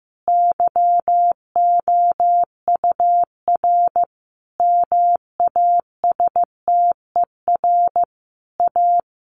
Morse code, heard as a keyed tone, about 14 wpm, 700 Hz. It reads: YOUR MASTER A
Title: Sentences from Top 1300 Words 15wpm